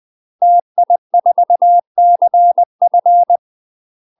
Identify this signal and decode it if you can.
Morse code, heard as a keyed tone, about 20 wpm, 700 Hz. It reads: TI4CF